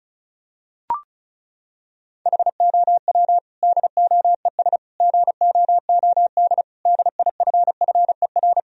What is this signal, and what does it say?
Morse code, heard as a keyed tone, about 36 wpm, 700 Hz. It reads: HOW DOES GOOD DIFFER
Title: Sentences from Top 1000 Words - Review Part 1 - 35wpm